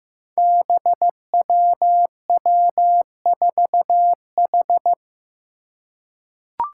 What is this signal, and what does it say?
Morse code, heard as a keyed tone, about 15 wpm, 700 Hz. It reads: BWW4H
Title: Five Challenging Letter-Number Combinations 15wpm